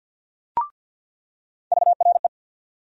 Morse code, heard as a keyed tone, about 49 wpm, 700 Hz. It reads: VRE
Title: USA Call Sign Postfixes - 3 Letters 50wpm